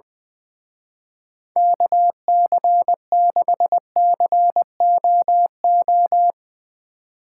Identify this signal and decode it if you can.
Morse code, heard as a keyed tone, about 21 wpm, 700 Hz. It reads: KC6COO
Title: Speed Racing - USA 2x3 Call Signs 20wpm